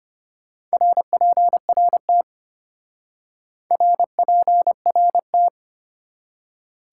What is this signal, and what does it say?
Morse code, heard as a keyed tone, about 33 wpm, 700 Hz. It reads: RPRT RPRT